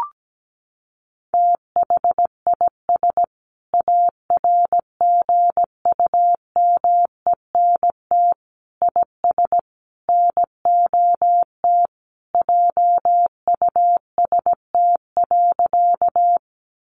Morse code, heard as a keyed tone, about 17 wpm, 700 Hz. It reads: THIS ARGUMENT IS NOT JUST.
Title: Sentences from Top 1400 Words 17wpm